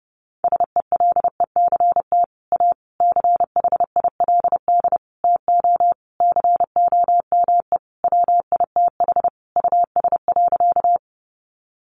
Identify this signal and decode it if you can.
Morse code, heard as a keyed tone, about 30 wpm, 700 Hz. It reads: SELECT A CHILD TO COME WITH US.
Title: Sentences from Top 1300 Words 30wpm